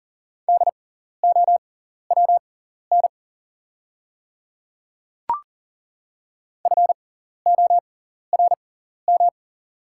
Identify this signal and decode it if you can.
Morse code, heard as a keyed tone, about 40 wpm, spaced as for 14 wpm, 690 Hz. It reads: DOWN FORM
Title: 6x Character Spacing - Top 200 Words - 40wpm